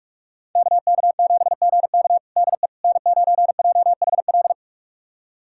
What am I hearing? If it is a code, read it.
Morse code, heard as a keyed tone, about 45 wpm, 700 Hz. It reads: KK7GK DE N9JHL
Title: USA Callsign DE Callsign - 5 Characters - 45wpm